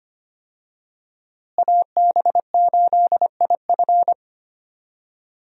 Morse code, heard as a keyed tone, about 25 wpm, 700 Hz. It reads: AB8IF
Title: USA Call Signs - 5 Characters - Repeated 2x - 25wpm